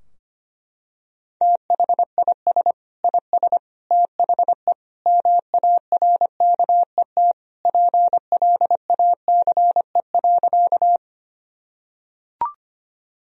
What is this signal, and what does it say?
Morse code, heard as a keyed tone, about 25 wpm, 700 Hz. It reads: THIS IS THE MARKET PLACE.
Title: Sentences from Top 1000 Words - Review Part 3 - 25wpm